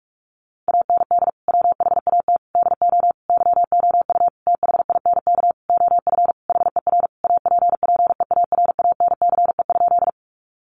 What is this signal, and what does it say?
Morse code, heard as a keyed tone, about 44 wpm, 700 Hz. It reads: AND WHAT DO YOU THINK OF HER APPEARANCE?
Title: Sentences from Top 1100 Words 45wpm